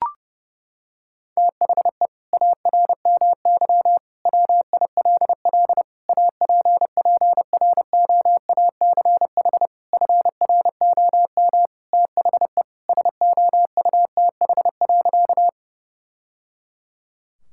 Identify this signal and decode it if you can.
Morse code, heard as a keyed tone, about 30 wpm, 700 Hz. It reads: THE ARMY WILL APPROACH FROM THE SOUTH.
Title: Sentences from Top 1000 Words - Review Part 6 - 30wpm